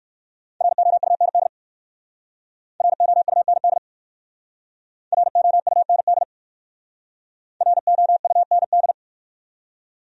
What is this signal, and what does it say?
Morse code, heard as a keyed tone, about 56 wpm, 700 Hz. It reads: ROUND ROUND ROUND ROUND
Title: Speed-Racing Top 200 Words 45wpm